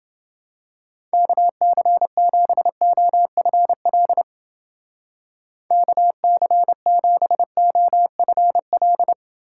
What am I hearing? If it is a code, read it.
Morse code, heard as a keyed tone, about 29 wpm, 700 Hz. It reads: KC7OFL KC7OFL